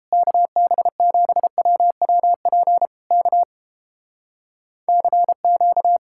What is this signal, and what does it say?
Morse code, heard as a keyed tone, about 31 wpm, 700 Hz. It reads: KB7WWP K CQ